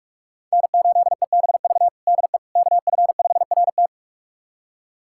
Morse code, heard as a keyed tone, about 41 wpm, 700 Hz. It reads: N8EBV DE KF5RT
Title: USA Callsign DE Callsign - 5 Characters - 45wpm